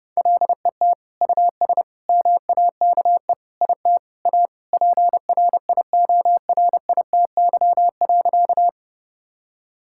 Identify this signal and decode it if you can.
Morse code, heard as a keyed tone, about 30 wpm, 700 Hz. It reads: LET US MAKE IT A PRIORITY.
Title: Sentences from Top 1600 Words 30wpm